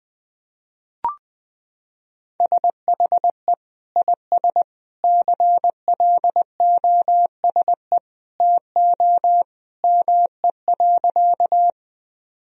Morse code, heard as a keyed tone, about 21 wpm, 700 Hz. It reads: SHE IS CLOSE TO ME.
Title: Sentences from Top 1000 Words - Review Part 1 - 20wpm